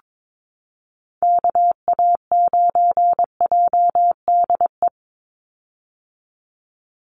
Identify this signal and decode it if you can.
Morse code, heard as a keyed tone, about 22 wpm, 700 Hz. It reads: KA9JDE
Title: USA Call Signs 22wpm